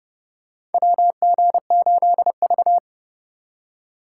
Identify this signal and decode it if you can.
Morse code, heard as a keyed tone, about 30 wpm, 700 Hz. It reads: WG8V
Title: USA Call Signs - 4 Characters - 4x Character Spacing - 30wpm